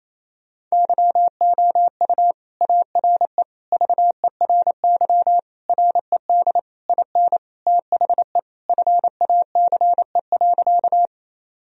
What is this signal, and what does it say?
Morse code, heard as a keyed tone, about 27 wpm, 700 Hz. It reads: YOU ARE VERY RED IN THE FACE.
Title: Sentences from Top 1000 Words - Review Part 2 - 28wpm